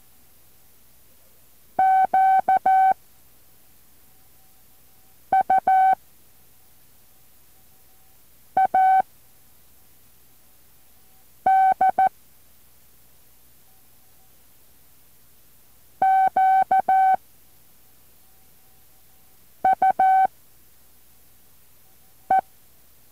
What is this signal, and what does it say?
Morse code, heard as a keyed tone, about 14 wpm, 760 Hz. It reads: Q U A D Q U E
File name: intl morse code2